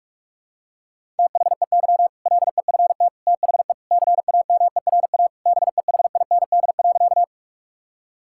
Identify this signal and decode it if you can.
Morse code, heard as a keyed tone, about 46 wpm, 700 Hz. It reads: THEY LEFT THE CAMERA BEHIND.
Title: Sentences from Top 1000 Words - Review Part 5 - 45wpm